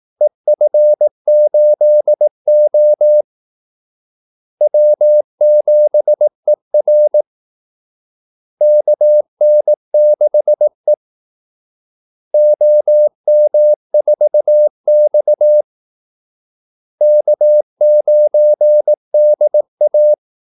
Morse code, heard as a keyed tone, about 18 wpm, 600 Hz. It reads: EF8O W7ER KN6E OM4X K9DA
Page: Kaldesignaler 18 wpm